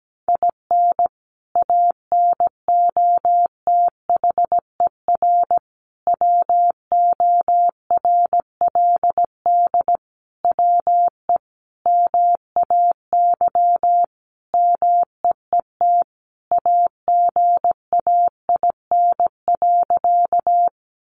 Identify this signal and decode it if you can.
Morse code, heard as a keyed tone, about 17 wpm, 700 Hz. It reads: IN ANOTHER WORLD WE MAY MEET AGAIN.
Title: Sentences from Top 1100 Words 17wpm